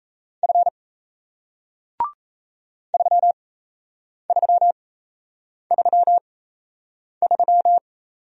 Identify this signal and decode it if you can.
Morse code, heard as a keyed tone, about 37 wpm, spaced as for 8 wpm, 700 Hz. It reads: F 3333